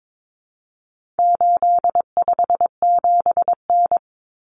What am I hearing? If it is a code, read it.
Morse code, heard as a keyed tone, about 22 wpm, 700 Hz. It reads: 857N
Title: Common Cut Numbers - 4-Digit 22wpm